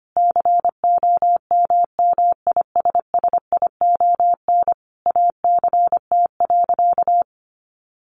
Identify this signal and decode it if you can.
Morse code, heard as a keyed tone, about 25 wpm, 700 Hz. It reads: COMMISSION ACT.
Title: Sentences from Top 1400 Words 25wpm